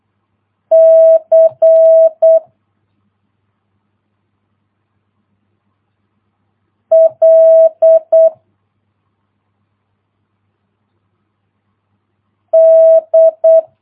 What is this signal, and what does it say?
Morse code, heard as a keyed tone, about 8 wpm, 650 Hz. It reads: C L D